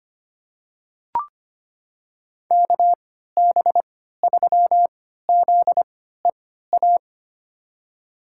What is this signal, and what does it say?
Morse code, heard as a keyed tone, about 25 wpm, spaced as for 14 wpm, 700 Hz. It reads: KB3ZEA
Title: USA Call Signs - 6 Characters - 3x Character Spacing - 25wpm